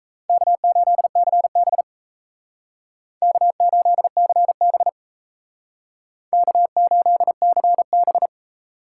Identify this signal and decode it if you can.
Morse code, heard as a keyed tone, about 36 wpm, 700 Hz. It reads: K8CB K8CB K8CB